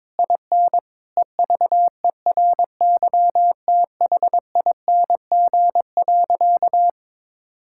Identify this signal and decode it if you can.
Morse code, heard as a keyed tone, about 22 wpm, 700 Hz. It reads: IN EVERYTHING.
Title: Sentences from Top 1600 Words 22wpm